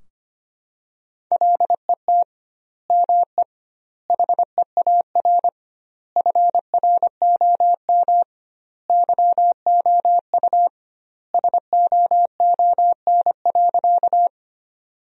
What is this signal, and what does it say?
Morse code, heard as a keyed tone, about 25 wpm, 700 Hz. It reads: LET ME HEAR FROM YOU SOON.
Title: 2x Word-Spacing - Sentences from Top 400 Words - 25wpm